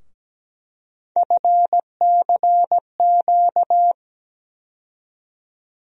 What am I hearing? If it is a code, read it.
Morse code, heard as a keyed tone, about 17 wpm, 700 Hz. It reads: FCQ
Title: USA Call Sign Postfixes - 3 Letters 17wpm